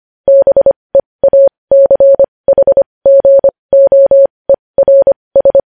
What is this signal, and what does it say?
Morse code, heard as a keyed tone, about 25 wpm, spaced as for 20 wpm, 550 Hz. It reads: BEACHGOERS
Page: news headlines in morse code at 20 wpm